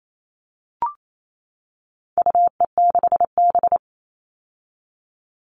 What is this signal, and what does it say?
Morse code, heard as a keyed tone, about 28 wpm, 700 Hz. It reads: UE6B
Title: All Cut Numbers - 4-Digit 28wpm